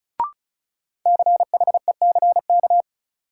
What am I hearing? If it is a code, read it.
Morse code, heard as a keyed tone, about 36 wpm, 700 Hz. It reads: CHECK
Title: Sentences from Top 1000 Words - Review Part 3 - New Word Drill - 35wpm